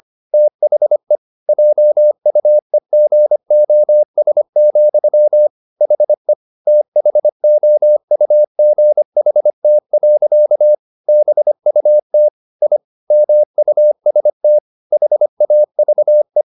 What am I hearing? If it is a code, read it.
Morse code, heard as a keyed tone, about 25 wpm, 600 Hz. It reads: THE JUEGOS, HE THOUGHT. BUT I MUST HAVE
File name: the old man and the sea part 1 25wpm